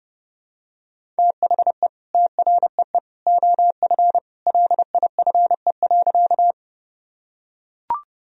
Morse code, heard as a keyed tone, about 30 wpm, 700 Hz. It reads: THE TREE OF LIFE.
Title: Sentences from Top 1000 Words - Review Part 1 - 30wpm